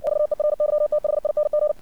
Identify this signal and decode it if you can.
Morse code, heard as a keyed tone, about 57 wpm, 600 Hz. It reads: VERYTHING
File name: archived / CRSnet